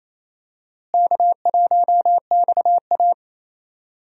Morse code, heard as a keyed tone, about 28 wpm, 700 Hz. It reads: K1XA
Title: USA Call Signs - 4 Characters - Repeated 2x - 28wpm